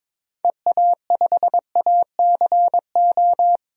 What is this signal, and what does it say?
Morse code, heard as a keyed tone, about 21 wpm, 700 Hz. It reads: EA5ACO
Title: DXCC-100 Call Signs - 6 Characters - 22wpm